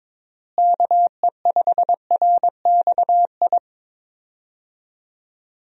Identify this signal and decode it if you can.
Morse code, heard as a keyed tone, about 22 wpm, 700 Hz. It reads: KE5RXI
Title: USA Call Signs - 6 Characters - Repeated 3x - 22wpm